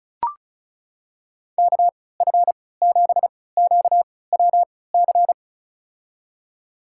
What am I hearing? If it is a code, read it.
Morse code, heard as a keyed tone, about 34 wpm, spaced as for 20 wpm, 700 Hz. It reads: KF7QWC
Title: USA Call Signs - 3x Character Spacing - 35wpm